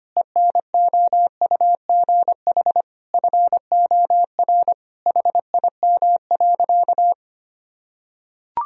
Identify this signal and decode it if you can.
Morse code, heard as a keyed tone, about 25 wpm, 700 Hz. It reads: ENOUGH FOR HIM.
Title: Sentences from Top 1000 Words - Review Part 4 - 25wpm